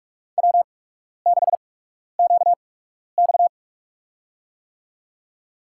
Morse code, heard as a keyed tone, about 45 wpm, spaced as for 12 wpm, 700 Hz. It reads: W6QX
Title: USA Call Signs - 4 Characters - 8x Character Spacing - 45wpm